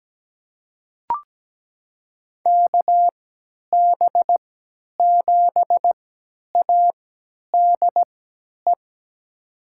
Morse code, heard as a keyed tone, about 17 wpm, spaced as for 10 wpm, 700 Hz. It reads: KB7ADE
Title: USA Call Signs - 6 Characters - 3x Character Spacing - 17wpm